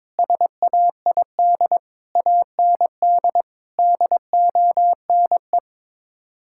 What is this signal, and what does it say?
Morse code, heard as a keyed tone, about 22 wpm, 700 Hz. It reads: SAID AND DONE
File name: practice / 22wpm / Sets of 6 Words - 22wpm